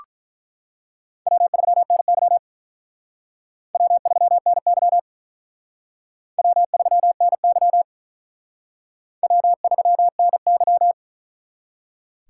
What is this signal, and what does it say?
Morse code, heard as a keyed tone, about 44 wpm, 700 Hz. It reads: W3NY W3NY W3NY W3NY
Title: Speed Racing - USA Call Signs 35wpm